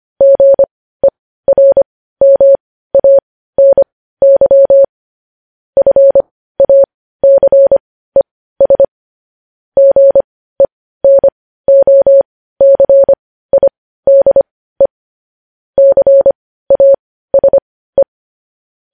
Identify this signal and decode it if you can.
Morse code, heard as a keyed tone, about 25 wpm, spaced as for 15 wpm, 550 Hz. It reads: GERMANY FACES GENOCIDE CASE
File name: podcast / News headlines-1x-25x15